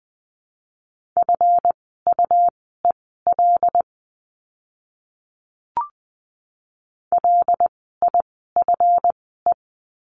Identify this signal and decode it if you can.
Morse code, heard as a keyed tone, about 20 wpm, spaced as for 15 wpm, 700 Hz. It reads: FUEL LIFE
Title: 139 Lesson 23f - Letter f - Words 20wpm